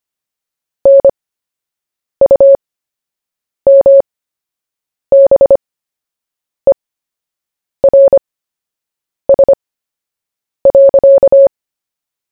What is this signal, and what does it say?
Morse code, heard as a keyed tone, about 25 wpm, spaced as for 7 wpm, 550 Hz. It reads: NUMBERS.
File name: cw / fists / Short-Story-1-7wpm